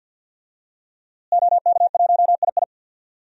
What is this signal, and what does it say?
Morse code, heard as a keyed tone, about 51 wpm, 700 Hz. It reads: OK1II
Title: DXCC-100 Call Signs - 5 Characters - Repeated 3x - 50wpm